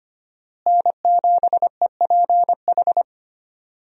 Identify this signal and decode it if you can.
Morse code, heard as a keyed tone, about 25 wpm, 700 Hz. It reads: N7EPH
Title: USA Call Signs - 5 Characters - Repeated 2x - 25wpm